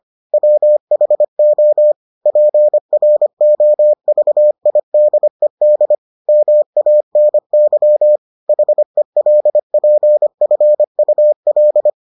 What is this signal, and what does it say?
Morse code, heard as a keyed tone, about 25 wpm, 600 Hz. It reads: WHO PROVIDED MANY HELPFUL